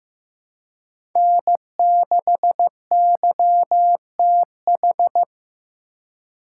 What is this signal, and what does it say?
Morse code, heard as a keyed tone, about 15 wpm, 700 Hz. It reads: N6YTH
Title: USA Call Signs - 5 Characters - 2x Character Spacing - 15wpm